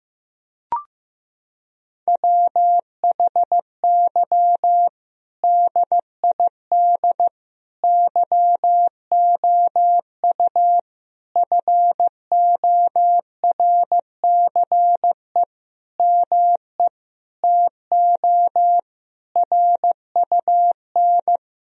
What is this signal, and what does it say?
Morse code, heard as a keyed tone, about 15 wpm, 700 Hz. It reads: WHY DID YOU FORCE ME TO RUN
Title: Sentences from Top 1000 Words - Review Part 3 - 15wpm